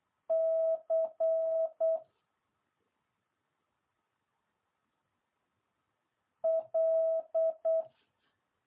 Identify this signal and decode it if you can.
Morse code, heard as a keyed tone, about 8 wpm, 650 Hz. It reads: C L